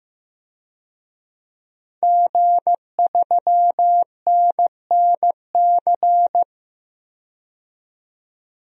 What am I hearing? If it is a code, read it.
Morse code, heard as a keyed tone, about 15 wpm, 700 Hz. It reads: G3NNC
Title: DXCC-100 Call Signs - 5 Characters - 15wpm